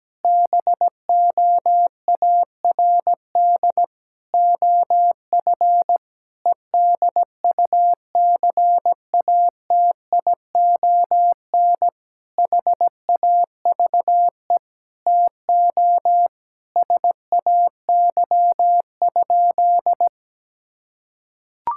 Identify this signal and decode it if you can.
Morse code, heard as a keyed tone, about 17 wpm, 700 Hz. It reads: BOARD OF EDUCATION HAVE TO SAY?
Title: Sentences from Top 1000 Words - Review Part 6 - 17wpm